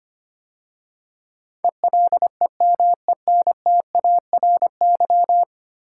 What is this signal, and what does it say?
Morse code, heard as a keyed tone, about 25 wpm, 700 Hz. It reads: ELEMENTARY
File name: 10 Letter Words - First 3 Letters Given - 25wpm